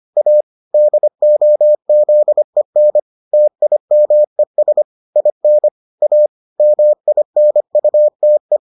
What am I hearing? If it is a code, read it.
Morse code, heard as a keyed tone, about 25 wpm, 600 Hz. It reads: A DOZEN TIMES IN A MINUTE